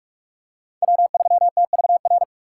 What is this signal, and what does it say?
Morse code, heard as a keyed tone, about 48 wpm, 700 Hz. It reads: W3TVR
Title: USA Call Signs - 2x Character Spacing - 45wpm